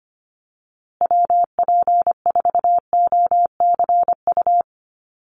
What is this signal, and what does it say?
Morse code, heard as a keyed tone, about 25 wpm, 700 Hz. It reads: WP4OCU
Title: USA Call Signs Repeated 3x 25wpm